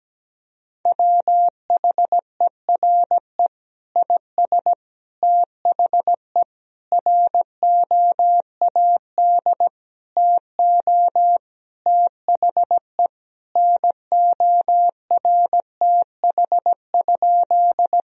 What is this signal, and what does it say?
Morse code, heard as a keyed tone, about 17 wpm, 700 Hz. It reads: WHERE IS THE ROAD TO THE NORTH?
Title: Sentences from Top 1000 Words - Review Part 2 - 17wpm